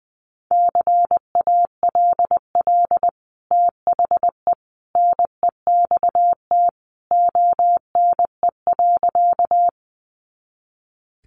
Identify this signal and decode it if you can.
Morse code, heard as a keyed tone, about 20 wpm, 700 Hz. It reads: CALL THE NEXT ONE.
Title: Sentences from Top 300 Words 20wpm